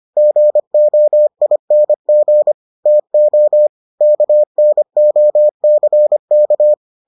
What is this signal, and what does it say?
Morse code, heard as a keyed tone, about 25 wpm, 600 Hz. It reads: GOING TO KNOCK